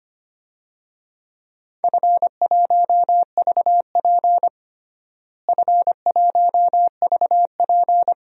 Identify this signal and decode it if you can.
Morse code, heard as a keyed tone, about 25 wpm, 700 Hz. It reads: F1VP F1VP
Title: DXCC-100 Call Signs - 4 Characters - Repeated 2x - 25wpm